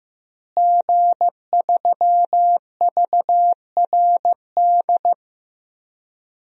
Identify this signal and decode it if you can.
Morse code, heard as a keyed tone, about 15 wpm, 700 Hz. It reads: G3VRD
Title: DXCC-100 Call Signs - 5 Characters - Repeated 2x - 15wpm